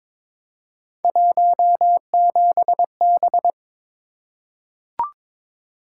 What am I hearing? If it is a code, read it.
Morse code, heard as a keyed tone, about 22 wpm, 700 Hz. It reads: 17B
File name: practice / 22wpm / Three Letter-Number Combinations - 22wpm